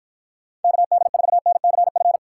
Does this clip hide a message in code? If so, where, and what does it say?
Yes: Morse code from 0.6 to 2.2 s, heard as a keyed tone, about 52 wpm, 700 Hz. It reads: KD4NCF